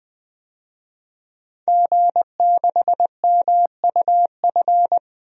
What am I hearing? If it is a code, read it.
Morse code, heard as a keyed tone, about 20 wpm, 700 Hz. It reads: G6MUF
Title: DXCC-100 Call Signs - 5 Characters - 20wpm